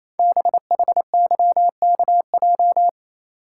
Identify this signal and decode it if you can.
Morse code, heard as a keyed tone, about 29 wpm, 690 Hz. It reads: BHYKJ
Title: Code Groups - 5 Letters - 28wpm